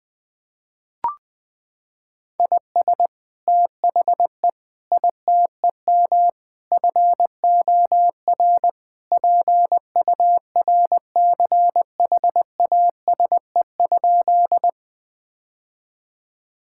Morse code, heard as a keyed tone, about 20 wpm, 700 Hz. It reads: IS THE ITEM FOR PURCHASE?
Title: Sentences from Top 1000 Words - Review Part 6 - 20wpm